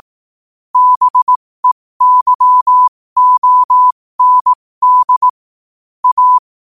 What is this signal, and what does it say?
Morse code, heard as a keyed tone, about 18 wpm, spaced as for 15 wpm, 1000 Hz. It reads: BEYOND A
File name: qotdfiles / 15wpm